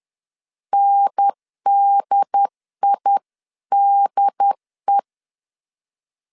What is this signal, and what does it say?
Morse code, heard as a keyed tone, about 10 wpm, 800 Hz. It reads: NDIDE
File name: 10wpm